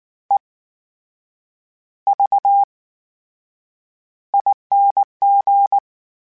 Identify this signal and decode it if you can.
Morse code, heard as a keyed tone, about 19 wpm, spaced as for 8 wpm, 800 Hz. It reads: EVING
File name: Special INGs / Special INGs-mp3s / VerySlowToFaster Ing @18